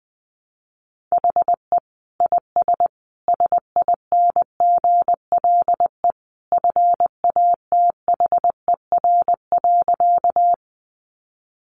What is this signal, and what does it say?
Morse code, heard as a keyed tone, about 20 wpm, 700 Hz. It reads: HE IS SINGLE FATHER.
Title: Sentences from Top 700 Words 20wpm